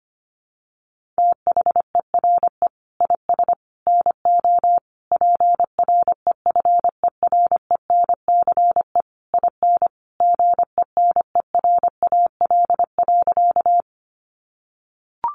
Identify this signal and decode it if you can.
Morse code, heard as a keyed tone, about 25 wpm, 700 Hz. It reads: THERE IS NO PREFERENCE IN GENERAL.
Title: Sentences from Top 1500 Words 25wpm